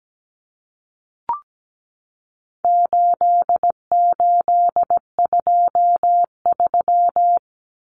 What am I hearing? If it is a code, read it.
Morse code, heard as a keyed tone, about 17 wpm, 700 Hz. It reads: 8823
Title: Numbers - 4-Digit 17wpm